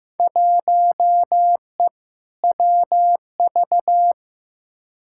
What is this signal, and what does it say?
Morse code, heard as a keyed tone, about 15 wpm, 700 Hz. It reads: 1E WV
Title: Field Day Exchanges Encore - 15wpm